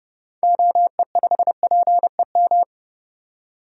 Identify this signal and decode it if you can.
Morse code, heard as a keyed tone, about 30 wpm, 700 Hz. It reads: OE5PEM
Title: DXCC-100 Call Signs - 6 Characters - 30wpm